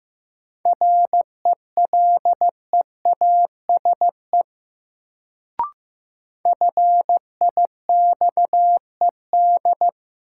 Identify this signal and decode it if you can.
Morse code, heard as a keyed tone, about 15 wpm, 700 Hz. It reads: RELEASE FIXED
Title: Sentences from Top 1100 Words - New Word Drill - 15wpm